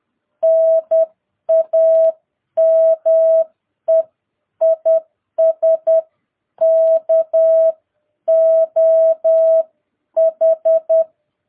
Morse code, heard as a keyed tone, about 10 wpm, 660 Hz. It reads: NAME IS KOH
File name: jh8cbh / 2022cw